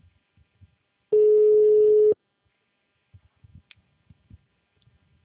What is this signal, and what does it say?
Ringback tone (Europe, 425 Hz).
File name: out- -2009